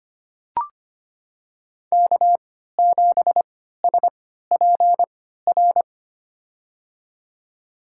Morse code, heard as a keyed tone, about 25 wpm, spaced as for 14 wpm, 700 Hz. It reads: K7SPR
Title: USA Call Signs - 5 Characters - 3x Character Spacing - 25wpm